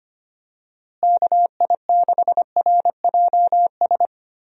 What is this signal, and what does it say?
Morse code, heard as a keyed tone, about 25 wpm, 700 Hz. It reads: KI6RJS
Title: USA Call Signs - 6 Characters - Repeated 2x - 25wpm